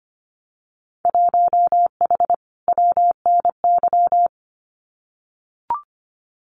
Morse code, heard as a keyed tone, about 25 wpm, 700 Hz. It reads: 1H WNY
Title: Winter Field Day Exchanges 25wpm